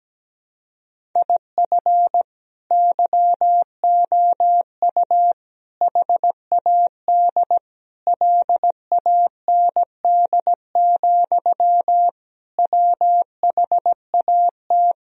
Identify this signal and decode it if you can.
Morse code, heard as a keyed tone, about 17 wpm, 700 Hz. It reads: IF YOU HAD LAND, WHAT
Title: Sentences from Top 1000 Words - Review Part 1 - 17wpm